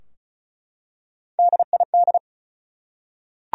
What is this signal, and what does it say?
Morse code, heard as a keyed tone, about 36 wpm, 700 Hz. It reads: DID
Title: Top 100 Words 35wpm